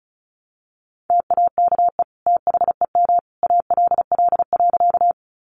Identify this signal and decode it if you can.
Morse code, heard as a keyed tone, about 34 wpm, 700 Hz. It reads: TAKE THEM ALL.
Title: Sentences from Top 200 Words 35wpm